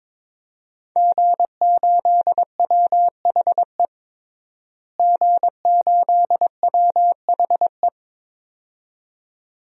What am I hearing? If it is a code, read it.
Morse code, heard as a keyed tone, about 22 wpm, 700 Hz. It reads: G8WHE G8WHE
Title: DXCC-100 Call Signs - 5 Characters - Repeated 2x - 22wpm